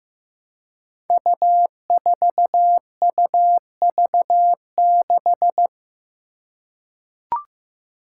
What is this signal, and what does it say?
Morse code, heard as a keyed tone, about 15 wpm, 700 Hz. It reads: U4UV6
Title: Five Challenging Letter-Number Combinations 15wpm